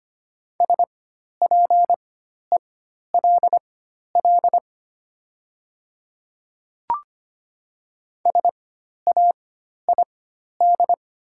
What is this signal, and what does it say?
Morse code, heard as a keyed tone, about 25 wpm, spaced as for 12 wpm, 700 Hz. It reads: SPELL SAID
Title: 4x Character Spacing - Top 200 Words - 25wpm